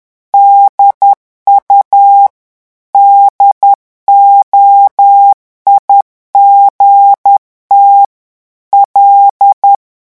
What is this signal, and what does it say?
Morse code, heard as a keyed tone, about 10 wpm, 800 Hz. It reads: DU DOIGT L
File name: morse mp3 / revision francais 10 mots minute